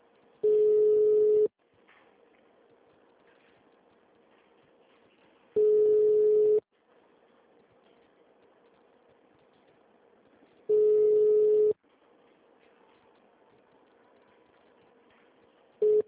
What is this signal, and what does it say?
Ringback tone (Europe, 425 Hz).
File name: out- -2002- - - .2